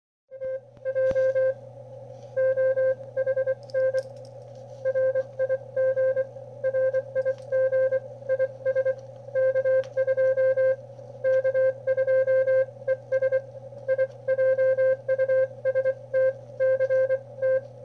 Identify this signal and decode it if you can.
Morse code, heard as a keyed tone, about 25 wpm, spaced as for 19 wpm, 520 Hz. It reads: TJOHN RIG RIG IS K2 K2 ES I JUST CT
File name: qrp / wb0jnr at 400mW with ae5x 02-23-2018